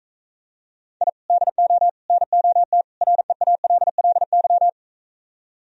Morse code, heard as a keyed tone, about 44 wpm, 700 Hz. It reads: I DO NOT REALLY